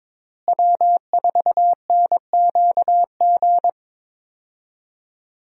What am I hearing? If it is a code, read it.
Morse code, heard as a keyed tone, about 22 wpm, 700 Hz. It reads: W4NQG
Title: USA Call Signs - 5 Characters - Repeated 2x - 22wpm